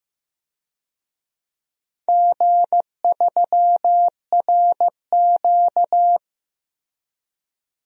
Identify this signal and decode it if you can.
Morse code, heard as a keyed tone, about 15 wpm, 700 Hz. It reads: G3RQ
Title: DXCC-100 Call Signs - 4 Characters - 15wpm